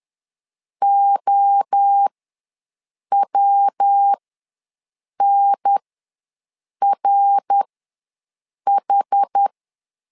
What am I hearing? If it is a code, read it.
Morse code, heard as a keyed tone, about 10 wpm, spaced as for 6 wpm, 800 Hz. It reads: OWNRH
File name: LEZIONI / 3 MWRO / 3 MWRO ws 10wpm 2310